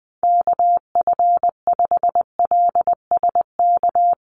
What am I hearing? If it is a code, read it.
Morse code, heard as a keyed tone, about 20 wpm, 700 Hz. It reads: KF5LSK
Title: USA Call Signs 20wpm